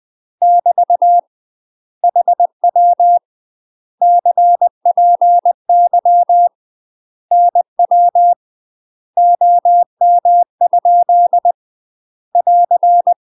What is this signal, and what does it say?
Morse code, heard as a keyed tone, about 20 wpm, 700 Hz. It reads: <BT> HW CPY NW OM? <AR>
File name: UK MASTER QSOs TRAINING / 20WPM / UK MASTER QSOs 0040